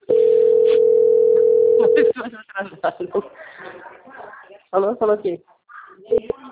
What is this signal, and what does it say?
Ringback tone (North America), 440 + 480 Hz, with speech.
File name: out- - - - - .5045